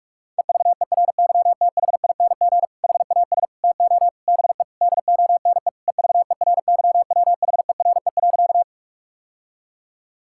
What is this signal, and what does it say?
Morse code, heard as a keyed tone, about 46 wpm, 700 Hz. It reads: EVERYTHING HAS TO BE DONE EVERYWHERE.
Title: Sentences from Top 1600 Words 45wpm